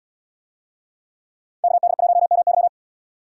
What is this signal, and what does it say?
Morse code, heard as a keyed tone, about 74 wpm, 690 Hz. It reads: KD1MY